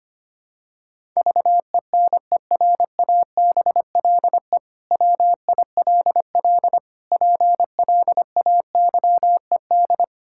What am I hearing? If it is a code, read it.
Morse code, heard as a keyed tone, about 25 wpm, 700 Hz. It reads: VENERABLE WILL PLAYED
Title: Pangram Challenge v2 - 25wpm